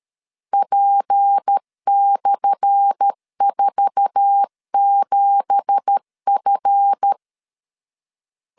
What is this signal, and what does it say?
Morse code, heard as a keyed tone, about 13 wpm, 800 Hz. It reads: P/47F
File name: LEZIONI / 10 SIGNS / 10A SLASH 12wpm 2314